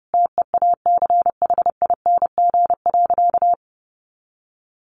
Morse code, heard as a keyed tone, about 30 wpm, 700 Hz. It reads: TEACHING.
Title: Sentences from Top 1100 Words 30wpm